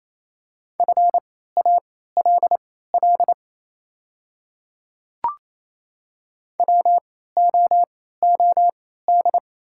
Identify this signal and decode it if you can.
Morse code, heard as a keyed tone, about 28 wpm, spaced as for 16 wpm, 700 Hz. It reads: FALL WOOD
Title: 3x Character Spacing - Top 500 Words - 28wpm